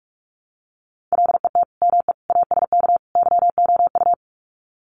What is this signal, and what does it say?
Morse code, heard as a keyed tone, about 44 wpm, 700 Hz. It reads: LET ME ASK YOU
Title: Sets of 4 Words 45wpm